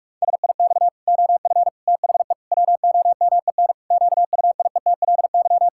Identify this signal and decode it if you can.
Morse code, heard as a keyed tone, about 46 wpm, 700 Hz. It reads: SIX OF THE WOMEN QUIETLY
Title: Pangram Challenge v2 - 45wpm